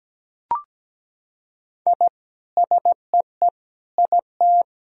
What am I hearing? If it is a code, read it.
Morse code, heard as a keyed tone, about 17 wpm, 700 Hz. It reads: I SEE IT
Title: Sentences from Top 1000 Words - Review Part 2 - 17wpm